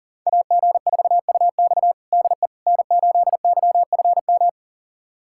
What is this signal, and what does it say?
Morse code, heard as a keyed tone, about 40 wpm, 700 Hz. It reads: AG4UX DE N8YFM
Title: USA Callsign DE Callsign - 5 Characters - 40wpm